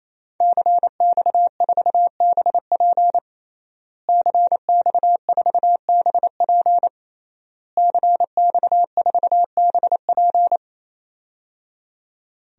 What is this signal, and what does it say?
Morse code, heard as a keyed tone, about 28 wpm, 700 Hz. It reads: CX4BP CX4BP CX4BP
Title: DXCC-100 Call Signs - 5 Characters - Repeated 3x - 28wpm